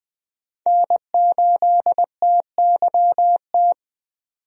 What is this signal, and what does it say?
Morse code, heard as a keyed tone, about 20 wpm, 700 Hz. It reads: N8TYT
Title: USA Call Signs - 5 Characters - 20wpm